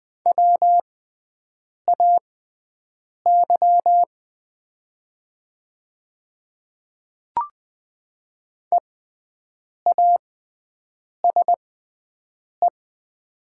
Morse code, heard as a keyed tone, about 20 wpm, spaced as for 7 wpm, 700 Hz. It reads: WAY EASE
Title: 6x Character Spacing - Top 300 Words - 20wpm